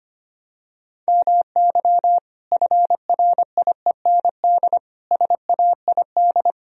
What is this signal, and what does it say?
Morse code, heard as a keyed tone, about 25 wpm, 700 Hz. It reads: MY FRIEND SAID
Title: Sentences from Top 1000 Words - Review Part 4 - 25wpm